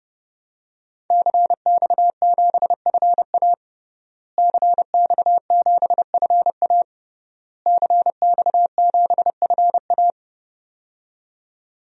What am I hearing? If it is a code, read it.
Morse code, heard as a keyed tone, about 32 wpm, 700 Hz. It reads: CX7FA CX7FA CX7FA
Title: DXCC-100 Call Signs - 5 Characters - Repeated 3x - 30wpm